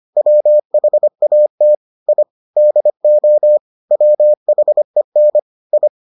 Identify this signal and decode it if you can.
Morse code, heard as a keyed tone, about 25 wpm, 600 Hz. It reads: WHAT I DO WHEN I